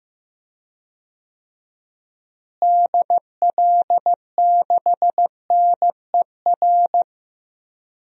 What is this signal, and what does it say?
Morse code, heard as a keyed tone, about 15 wpm, 700 Hz. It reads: DL6NER
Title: DXCC-100 Call Signs - 6 Characters - Repeated 3x - 15wpm